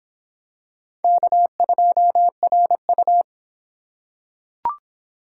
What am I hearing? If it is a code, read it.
Morse code, heard as a keyed tone, about 26 wpm, 700 Hz. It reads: K2RU